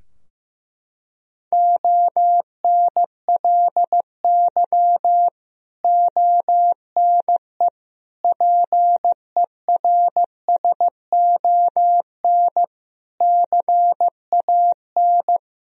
Morse code, heard as a keyed tone, about 16 wpm, 700 Hz. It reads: ONLY ONE PERSON CAN
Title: Sentences from Top 1000 Words - Review Part 3 - 15wpm